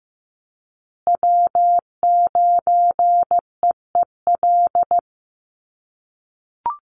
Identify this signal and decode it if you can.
Morse code, heard as a keyed tone, about 15 wpm, 700 Hz. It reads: W9EEL
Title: USA Call Signs Repeated 2x 15wpm